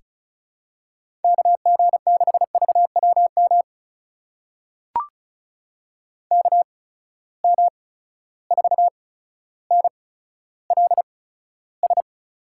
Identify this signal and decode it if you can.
Morse code, heard as a keyed tone, about 36 wpm, spaced as for 15 wpm, 700 Hz. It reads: KG6VWM KM4NLS
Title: USA Call Signs - 6 Characters - 8x Character Spacing - 35wpm